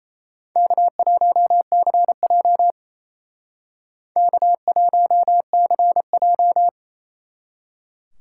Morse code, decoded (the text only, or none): K1CJ K1CJ